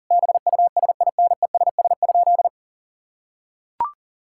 Morse code, heard as a keyed tone, about 40 wpm, 700 Hz. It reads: BUSINESS?
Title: Sentences from Top 1000 Words - Review Part 6 - 40wpm